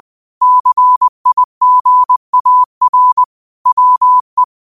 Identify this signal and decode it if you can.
Morse code, heard as a keyed tone, about 20 wpm, 1000 Hz. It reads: CIGAR WE
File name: qotdfiles / 20wpm